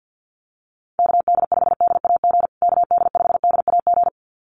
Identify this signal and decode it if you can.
Morse code, heard as a keyed tone, about 50 wpm, 700 Hz. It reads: KD5DAG KD5DAG
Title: USA Call Signs Repeated 2x 50wpm